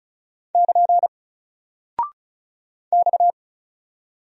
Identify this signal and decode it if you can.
Morse code, heard as a keyed tone, about 36 wpm, 700 Hz. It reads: <KN> X
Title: Single Character (Letter:Number:Punctuation:Prosign) Alt 35wpm